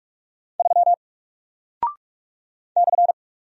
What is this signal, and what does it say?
Morse code, heard as a keyed tone, about 45 wpm, 700 Hz. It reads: 3 /
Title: Single Character (Letter:Number:Punctuation:Prosign) Alt 45wpm